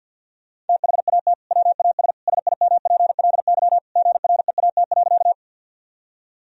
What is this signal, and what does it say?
Morse code, heard as a keyed tone, about 50 wpm, 700 Hz. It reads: THAT WAS SIMPLY GREAT.
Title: Sentences from Top 1000 Words - Review Part 4 - 50wpm